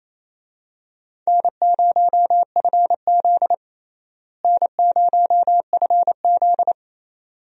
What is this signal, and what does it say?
Morse code, heard as a keyed tone, about 28 wpm, 700 Hz. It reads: N0FZ N0FZ
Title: USA Call Signs - 4 Characters - Repeated 2x - 28wpm